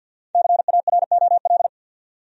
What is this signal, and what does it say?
Morse code, heard as a keyed tone, about 52 wpm, 700 Hz. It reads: CAROL
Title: USA Common Names - Max 5 Letters - 50wpm